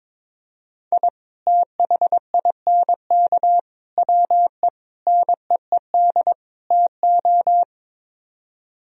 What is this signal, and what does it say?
Morse code, heard as a keyed tone, about 22 wpm, 700 Hz. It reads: I THINK WE NEED TO
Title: Speed-Racing - Sets of 5 Words 22wpm